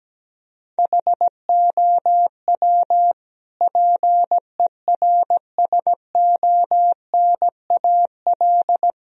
Morse code, heard as a keyed tone, about 17 wpm, 700 Hz. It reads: HOW PERSONAL
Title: Sentences from Top 1000 Words - Review Part 4 - 17wpm